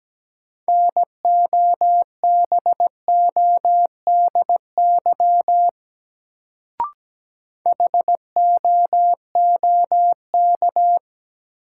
Morse code, heard as a keyed tone, about 17 wpm, 700 Hz. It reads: NOBODY HOOK
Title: Sentences from Top 1600 Words - New Word Drill Alt - 17wpm